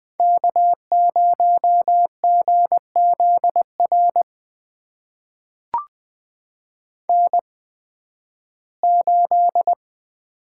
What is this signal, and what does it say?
Morse code, heard as a keyed tone, about 20 wpm, spaced as for 13 wpm, 700 Hz. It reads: K0GZR N8
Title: USA Call Signs - 8x Character Spacing - 20wpm